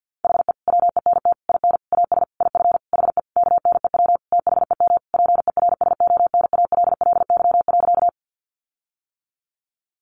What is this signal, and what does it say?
Morse code, heard as a keyed tone, about 50 wpm, 700 Hz. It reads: HE WENT IN AS IF HE KNEW THEM PERSONALLY.
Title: Sentences from Top 1400 Words 50wpm